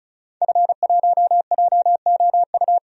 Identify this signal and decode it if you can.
Morse code, heard as a keyed tone, about 35 wpm, 700 Hz. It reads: F1JOU
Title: DXCC-100 Call Signs - 5 Characters - 35wpm